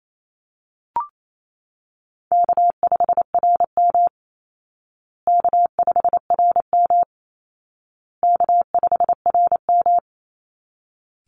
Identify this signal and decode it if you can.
Morse code, heard as a keyed tone, about 28 wpm, 700 Hz. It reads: K5RM K5RM K5RM
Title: Easy Call Signs Repeated 3x - 28wpm